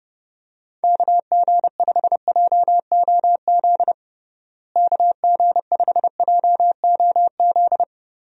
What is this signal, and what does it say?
Morse code, heard as a keyed tone, about 30 wpm, 700 Hz. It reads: KG5JOZ KG5JOZ
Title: USA Call Signs - 6 Characters - Repeated 2x - 30wpm